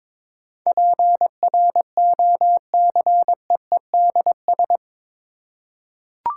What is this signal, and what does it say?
Morse code, heard as a keyed tone, about 22 wpm, 700 Hz. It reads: PROCEEDS
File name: practice / 22wpm / Eight Letter Words - 22wpm